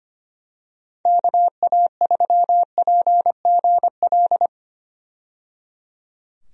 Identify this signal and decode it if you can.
Morse code, heard as a keyed tone, about 25 wpm, 700 Hz. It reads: KA3PGL
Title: USA Call Signs - 6 Characters - 25wpm